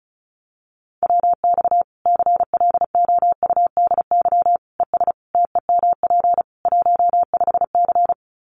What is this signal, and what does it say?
Morse code, heard as a keyed tone, about 34 wpm, 700 Hz. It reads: WX CLOUDY ES TEMP 15C
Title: Common QSO Phrases 35wpm